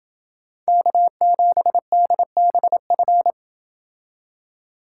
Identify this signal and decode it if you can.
Morse code, heard as a keyed tone, about 28 wpm, 700 Hz. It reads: K7DBF